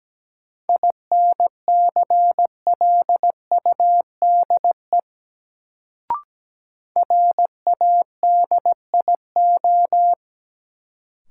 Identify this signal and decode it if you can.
Morse code, heard as a keyed tone, about 17 wpm, 700 Hz. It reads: INCLUDE RADIO
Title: Sentences from Top 1000 Words - Review Part 3 - New Word Drill - 17wpm